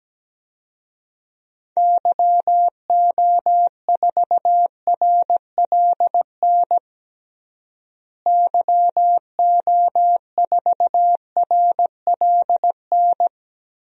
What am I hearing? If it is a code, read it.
Morse code, heard as a keyed tone, about 17 wpm, 700 Hz. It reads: YO4RLN YO4RLN
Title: DXCC-100 Call Signs - 6 Characters - Repeated 2x - 17wpm